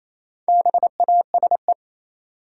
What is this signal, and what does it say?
Morse code, heard as a keyed tone, about 28 wpm, 700 Hz. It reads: BASE
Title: Sentences from Top 1000 Words - Review Part 1 - New Word Drill - 28wpm